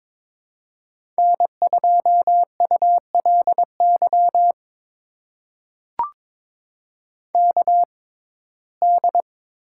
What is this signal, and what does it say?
Morse code, heard as a keyed tone, about 23 wpm, spaced as for 16 wpm, 700 Hz. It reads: N2ULY KD
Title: USA Call Signs - 5 Characters - 6x Character Spacing - 22wpm